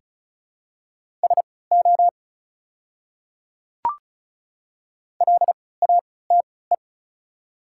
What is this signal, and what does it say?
Morse code, heard as a keyed tone, about 36 wpm, spaced as for 20 wpm, 700 Hz. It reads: SO LATE
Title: 3x Character Spacing - Top 300 Words - 35wpm